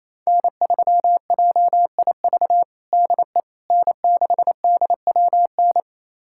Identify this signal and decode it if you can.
Morse code, heard as a keyed tone, about 28 wpm, 700 Hz. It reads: N3JIV DE N6DWN
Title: USA Callsign DE Callsign - 5 Characters - 28wpm